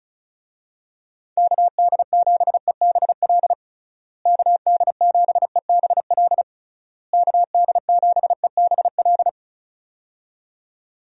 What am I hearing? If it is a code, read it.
Morse code, heard as a keyed tone, about 36 wpm, 700 Hz. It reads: KD7EBL KD7EBL KD7EBL
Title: USA Call Signs - 6 Characters - Repeated 3x - 35wpm